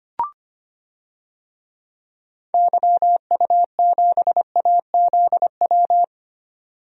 Morse code, heard as a keyed tone, about 25 wpm, 700 Hz. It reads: YU7AZW
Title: DXCC-100 Call Signs - 6 Characters - 25wpm